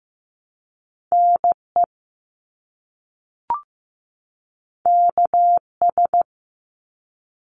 Morse code, heard as a keyed tone, about 15 wpm, 700 Hz. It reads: NE KS
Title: US States 15wpm